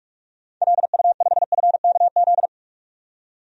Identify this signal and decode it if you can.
Morse code, heard as a keyed tone, about 46 wpm, 700 Hz. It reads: LU5FKZ
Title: DXCC-100 Call Signs - 6 Characters - 45wpm